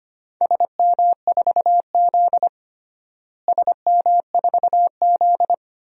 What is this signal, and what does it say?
Morse code, heard as a keyed tone, about 25 wpm, 700 Hz. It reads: SM4Z SM4Z
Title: DXCC-100 Call Signs - 4 Characters - Repeated 2x - 25wpm